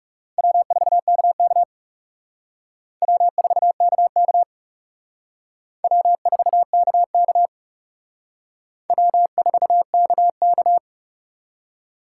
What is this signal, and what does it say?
Morse code, heard as a keyed tone, about 36 wpm, 700 Hz. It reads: W4KK W4KK W4KK W4KK
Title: Speed Racing - USA 1x2 Call Signs 30wpm